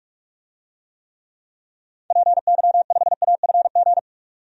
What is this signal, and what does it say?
Morse code, heard as a keyed tone, about 47 wpm, 700 Hz. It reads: PY5AFG
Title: DXCC-100 Call Signs - 6 Characters - 45wpm